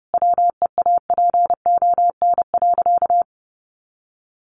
Morse code, heard as a keyed tone, about 30 wpm, 700 Hz. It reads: WEAPON.
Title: A Sound Of Thunder 30wpm